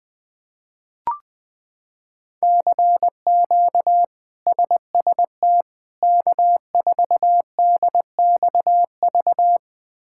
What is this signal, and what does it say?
Morse code, heard as a keyed tone, about 20 wpm, 700 Hz. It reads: CQ SST K4DXV
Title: K1USN SST QSO Practice - 20wpm